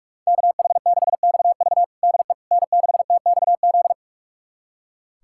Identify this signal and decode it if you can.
Morse code, heard as a keyed tone, about 42 wpm, 700 Hz. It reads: KH6XV DE N6TXZ
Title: USA Callsign DE Callsign - 5 Characters - 45wpm